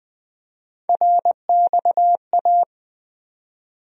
Morse code, heard as a keyed tone, about 20 wpm, 700 Hz. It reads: RXA
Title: USA Call Sign Postfixes - 3 Letters 20wpm